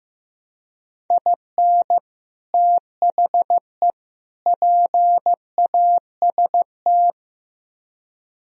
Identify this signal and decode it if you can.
Morse code, heard as a keyed tone, about 16 wpm, 700 Hz. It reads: IN THE PAST
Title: Speed-Racing - Sets of 3 Words 15wpm